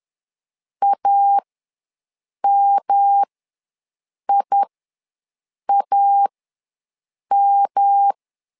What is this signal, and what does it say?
Morse code, heard as a keyed tone, about 10 wpm, spaced as for 6 wpm, 800 Hz. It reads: AMIAM
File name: LEZIONI / 3 MWRO / 3 M ws 10wpm 2121